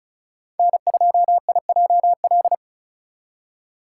Morse code, heard as a keyed tone, about 36 wpm, 700 Hz. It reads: N2IJL
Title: USA Call Signs - 5 Characters - 3x Character Spacing - 35wpm